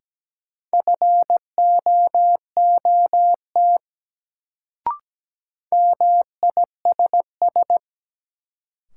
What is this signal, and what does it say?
Morse code, heard as a keyed tone, about 17 wpm, 700 Hz. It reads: FOOT MISS
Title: Sentences from Top 500 Words - New Word Drill - 17wpm